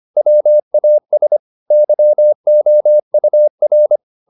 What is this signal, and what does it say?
Morse code, heard as a keyed tone, about 25 wpm, 600 Hz. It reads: WAS YOUR
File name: the old man and the sea part 1 25wpm